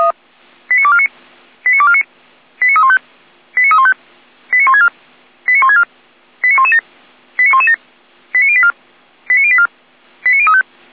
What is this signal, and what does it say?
Five-tone selective call (ZVEI), heard as quick runs of five tones, tones of about 70 ms. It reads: 89249 89249 89315 89315 89164 89164 89117 89117 89084 89084 89035
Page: /455/SDR/ZVEI/